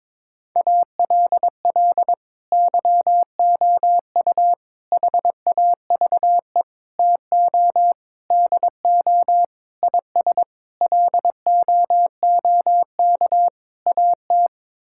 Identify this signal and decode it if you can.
Morse code, heard as a keyed tone, about 22 wpm, 700 Hz. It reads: ALL YOU HAVE TO DO IS LOOK AT
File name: practice / 22wpm / Sets of 8 Words - 22wpm